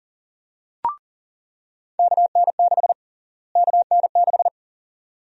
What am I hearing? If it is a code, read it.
Morse code, heard as a keyed tone, about 37 wpm, 700 Hz. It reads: KN6 KN6
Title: USA Call Sign Prefixes - Repeated 2x - 40wpm